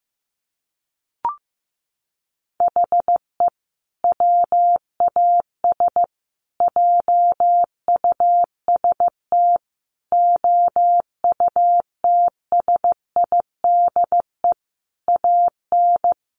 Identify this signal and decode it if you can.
Morse code, heard as a keyed tone, about 15 wpm, 700 Hz. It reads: HE WAS JUST OUTSIDE AN
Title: Sentences from Top 1100 Words 15wpm